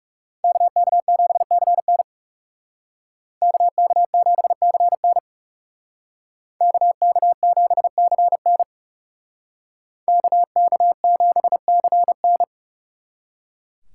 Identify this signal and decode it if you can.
Morse code, heard as a keyed tone, about 37 wpm, 700 Hz. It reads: KK7CN KK7CN KK7CN KK7CN
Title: Speed Racing - USA 2x2 Call Signs 30wpm